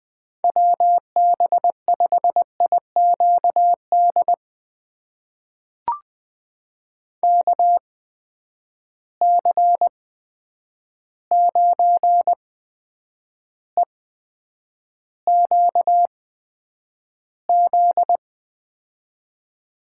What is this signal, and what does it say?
Morse code, heard as a keyed tone, about 20 wpm, spaced as for 9 wpm, 700 Hz. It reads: WB5IQD KC9EQZ
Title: USA Call Signs - 6 Characters - 8x Character Spacing - 20wpm